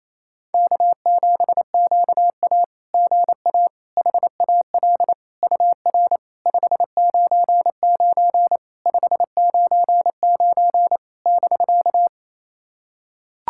Morse code, heard as a keyed tone, about 28 wpm, 700 Hz. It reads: K7QA GA HAL UR 599 599 <BK>
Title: SOTA QSO Practice 28wpm